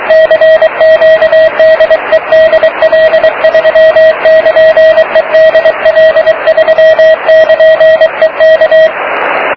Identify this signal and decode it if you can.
Morse code, heard as a keyed tone, about 25 wpm, 640 Hz, with radio noise. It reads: CQ DE DL3<KN>E DL3<KN>E K